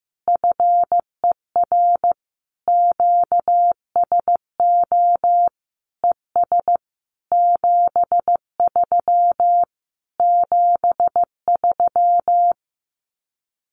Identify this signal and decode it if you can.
Morse code, heard as a keyed tone, about 14 wpm, 700 Hz. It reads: FER QSO ES 73 73
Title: Common QSO Phrases Encore 15wpm